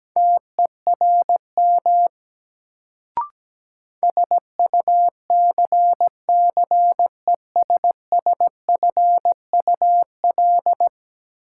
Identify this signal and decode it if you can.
Morse code, heard as a keyed tone, about 17 wpm, 700 Hz. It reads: TERM SUCCESSFUL
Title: Sentences from Top 900 Words - New Word Drill - 17wpm